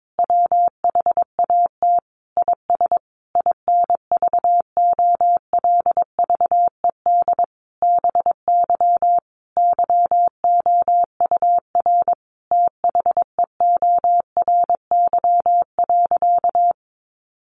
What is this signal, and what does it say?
Morse code, heard as a keyed tone, about 22 wpm, 700 Hz. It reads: WHAT IS INVOLVED BY YOUR THEORY.
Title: Sentences from Top 800 Words 22wpm